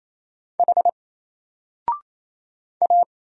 Morse code, heard as a keyed tone, about 27 wpm, 700 Hz. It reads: H A
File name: Single-Letter-Number-28wpm